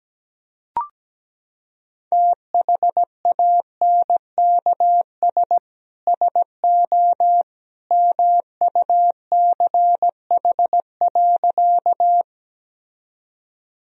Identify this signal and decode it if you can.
Morse code, heard as a keyed tone, about 17 wpm, 700 Hz. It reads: THANKS SO MUCH.
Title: Sentences from Top 1000 Words - Review Part 3 - 17wpm